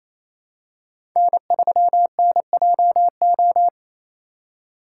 Morse code, heard as a keyed tone, about 28 wpm, 700 Hz. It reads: N3NJO
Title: USA Call Signs - 5 Characters - 2x Character Spacing - 28wpm